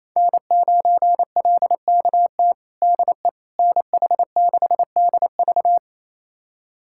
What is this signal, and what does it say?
Morse code, heard as a keyed tone, about 27 wpm, 700 Hz. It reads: N9LKT DE NH6DV
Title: USA Callsign DE Callsign - 5 Characters - 28wpm